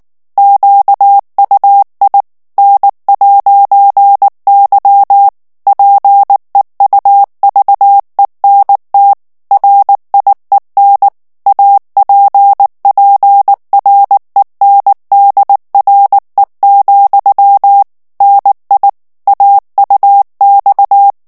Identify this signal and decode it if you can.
Morse code, heard as a keyed tone, about 19 wpm, 800 Hz. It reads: QUI N'Y PEUVENT RIEN APPRENDRE, NI AUX
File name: CW / FICHIERS-MP3 / FRANCAIS TEXTE CLAIR 2 VITESSE 18 MOTS MINUTES